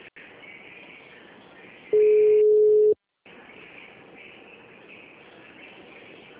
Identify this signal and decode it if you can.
Ringback tone (Europe, 425 Hz).